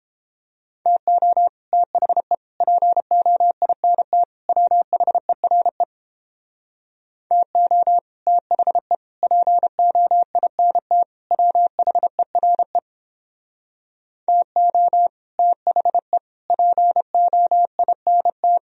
Morse code, heard as a keyed tone, about 29 wpm, 700 Hz. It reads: TO THE POINT WHERE TO THE POINT WHERE TO THE POINT